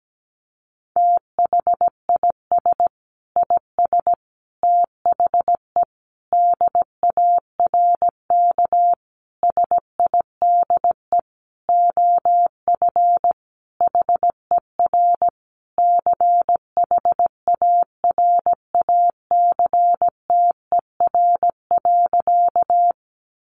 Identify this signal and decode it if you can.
Morse code, heard as a keyed tone, about 17 wpm, 700 Hz. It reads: THIS IS THE DARK SIDE OF HER CHARACTER.
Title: Sentences from Top 1200 Words 17wpm